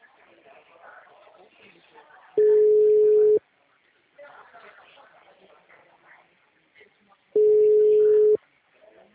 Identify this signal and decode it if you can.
Ringback tone (Europe, 425 Hz), with speech.